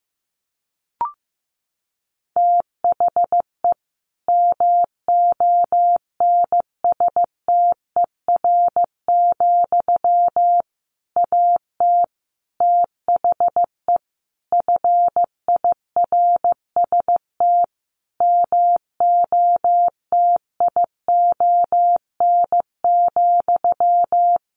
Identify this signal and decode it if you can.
Morse code, heard as a keyed tone, about 15 wpm, 700 Hz. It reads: THE MONSTER, AT THE FIRST MOTION,
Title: A Sound Of Thunder 15wpm